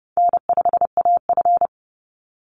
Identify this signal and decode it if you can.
Morse code, heard as a keyed tone, about 29 wpm, 700 Hz. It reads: N5AF
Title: Easy Call Signs Repeated 2x - 30wpm